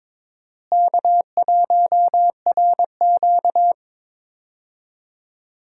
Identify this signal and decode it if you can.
Morse code, heard as a keyed tone, about 21 wpm, 700 Hz. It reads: K1RQ
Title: USA Call Signs - 4 Characters - Repeated 2x - 22wpm